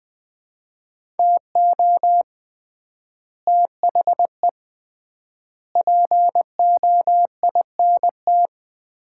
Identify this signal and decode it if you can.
Morse code, heard as a keyed tone, about 20 wpm, 700 Hz. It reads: TO THE POINT
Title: 3x Word-Spacing - Sets of 4 Words - 20wpm